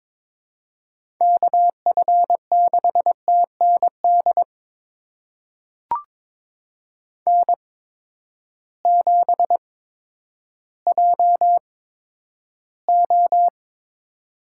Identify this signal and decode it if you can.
Morse code, heard as a keyed tone, about 22 wpm, spaced as for 11 wpm, 700 Hz. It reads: KF6TND N7JO
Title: USA Call Signs - 8x Character Spacing - 22wpm